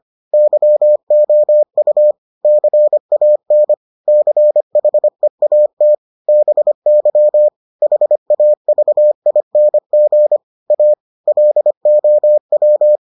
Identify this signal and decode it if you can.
Morse code, heard as a keyed tone, about 25 wpm, 600 Hz. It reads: YOU CAN CHEAT BY HAVING A LOW